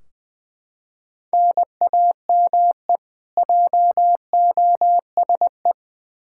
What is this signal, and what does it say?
Morse code, heard as a keyed tone, about 20 wpm, 700 Hz. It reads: NAME JOSE
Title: Practice QSO - Exchange 1 - Beginning - 20wpm